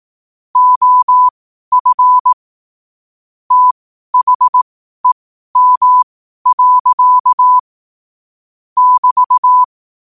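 Morse code, heard as a keyed tone, about 18 wpm, spaced as for 12 wpm, 1000 Hz. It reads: OF THEM. <BT>
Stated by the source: 12 wpm Morse code.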